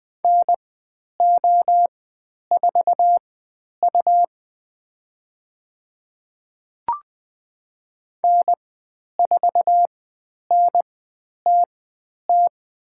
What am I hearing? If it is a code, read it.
Morse code, heard as a keyed tone, about 20 wpm, spaced as for 10 wpm, 700 Hz. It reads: NO4U N4NTT
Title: 102 Lesson 17g - Letter u - Call Signs 20wpm